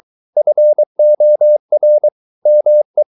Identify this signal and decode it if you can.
Morse code, heard as a keyed tone, about 23 wpm, 600 Hz. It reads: FOR ME